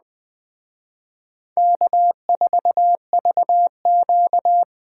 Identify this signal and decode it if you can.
Morse code, heard as a keyed tone, about 20 wpm, 700 Hz. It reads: K4VQ
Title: Speed Racing - USA 1x2 Call Signs 17wpm